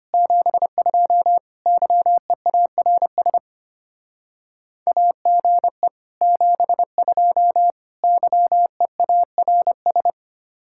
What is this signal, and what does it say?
Morse code, heard as a keyed tone, about 31 wpm, 700 Hz. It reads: 72 YEARS AGE 72 YEARS